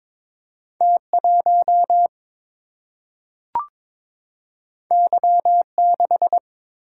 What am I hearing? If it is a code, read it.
Morse code, heard as a keyed tone, about 23 wpm, 700 Hz. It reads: T 1 Y 6
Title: Single-Letter Single-Number 22wpm